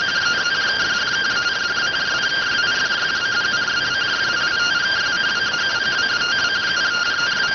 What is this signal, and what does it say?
Two-tone teleprinter signal, about 170 Hz shift.